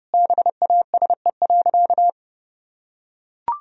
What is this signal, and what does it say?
Morse code, heard as a keyed tone, about 30 wpm, 700 Hz. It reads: BASE.
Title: Sentences from Top 1000 Words - Review Part 1 - 30wpm